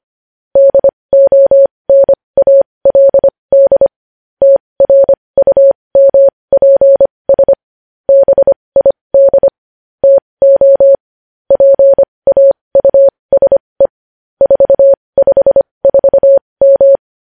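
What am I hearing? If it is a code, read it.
Morse code, heard as a keyed tone, about 25 wpm, spaced as for 20 wpm, 550 Hz. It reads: DONALD TRUMPS BID TO PAUSE 454M
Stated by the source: Morse code news headlines practice at 20 wpm.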